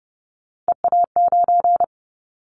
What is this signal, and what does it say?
Morse code, heard as a keyed tone, about 30 wpm, 700 Hz. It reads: EA9
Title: Callsign Prefixes for Top 100 DXCC - 30wpm